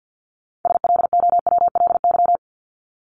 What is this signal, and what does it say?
Morse code, heard as a keyed tone, about 50 wpm, 700 Hz. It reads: SLOWLY
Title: LY Suffix - Top 100 Words with Base Words 50wpm